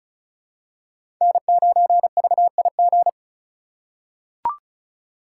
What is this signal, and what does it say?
Morse code, heard as a keyed tone, about 36 wpm, 700 Hz. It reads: N9VIG
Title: USA Call Signs - 4x Character Spacing - 35wpm